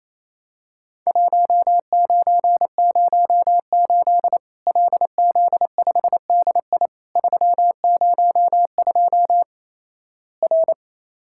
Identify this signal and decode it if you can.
Morse code, heard as a keyed tone, about 28 wpm, 700 Hz. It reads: 1908 LZ5DI 302 R
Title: Worked All Europe DX Contest - QTC Exchange - 28wpm